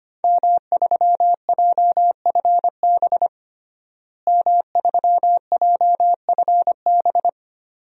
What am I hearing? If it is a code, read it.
Morse code, heard as a keyed tone, about 25 wpm, 700 Hz. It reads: M3JFB M3JFB
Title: DXCC-100 Call Signs - 5 Characters - Repeated 2x - 25wpm